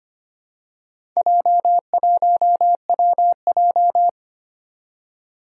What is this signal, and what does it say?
Morse code, heard as a keyed tone, about 25 wpm, 700 Hz. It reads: J1WJ
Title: Four Challenging Letter-Number Combinations 25wpm